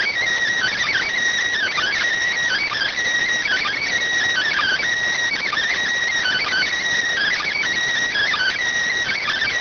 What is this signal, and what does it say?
Emergency Alert System (SAME) tones (header bursts), heard as harsh buzzing bursts. It is heard from 0.2 to 8.8 s.